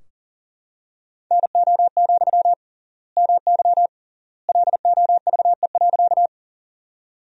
Morse code, heard as a keyed tone, about 40 wpm, 700 Hz. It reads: NO, MY LOVE.
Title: 3x Word-Spacing - Sentences from Top 400 Words - 40wpm